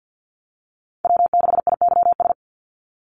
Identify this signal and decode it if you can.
Morse code, heard as a keyed tone, about 49 wpm, 700 Hz. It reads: W6IYS
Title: NA SOTA Chaser Callsigns 50wpm